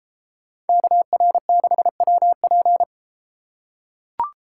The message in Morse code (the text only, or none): KR6WP